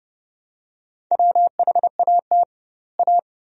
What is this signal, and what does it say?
Morse code, heard as a keyed tone, about 32 wpm, 700 Hz. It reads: WHAT A
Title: 2x Word-Spacing - Sentences from Top 400 Words - 30wpm